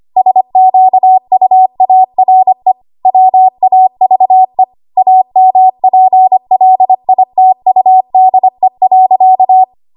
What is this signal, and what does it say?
Morse code, heard as a keyed tone, about 25 wpm, 750 Hz. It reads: SQUARE WAVE AMPLITUDE.
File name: Archive / 25 WPM / 25WPM